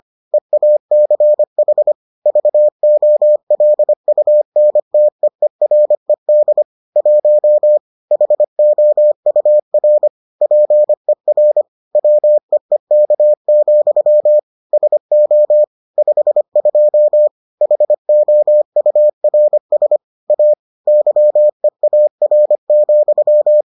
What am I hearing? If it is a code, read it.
Morse code, heard as a keyed tone, about 25 wpm, 600 Hz. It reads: EACH VOLUNTEERED 1 HOUR PER WEEK, SO 52 HOURS A YEAR,